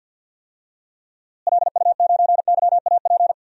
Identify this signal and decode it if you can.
Morse code, heard as a keyed tone, about 51 wpm, 700 Hz. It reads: LU9YAP
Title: DXCC-100 Call Signs - 6 Characters - Repeated 3x - 50wpm